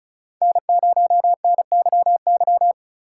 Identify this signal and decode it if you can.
Morse code, heard as a keyed tone, about 36 wpm, 700 Hz. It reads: N0NYY
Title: USA Call Signs - 5 Characters - 4x Character Spacing - 35wpm